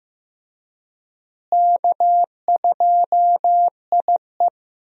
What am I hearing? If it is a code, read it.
Morse code, heard as a keyed tone, about 15 wpm, 700 Hz. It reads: K2IE
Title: USA Call Signs - 4 Characters - Repeated 3x - 15wpm